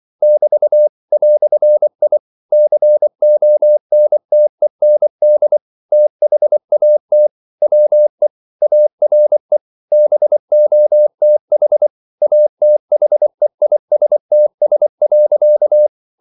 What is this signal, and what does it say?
Morse code, heard as a keyed tone, about 24 wpm, 600 Hz. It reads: <BT> "I CONTEND THAT WE ARE BOTH ATHEISTS.
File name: CW / Practice / EnglishQuotations0424